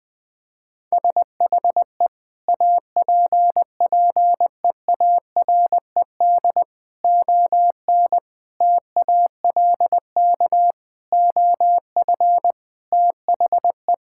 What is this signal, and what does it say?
Morse code, heard as a keyed tone, about 20 wpm, 700 Hz. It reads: SHE APPEARED ON TALK OF THE
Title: Sentences with 5-Word Phrases - Spoken Beginning - 20wpm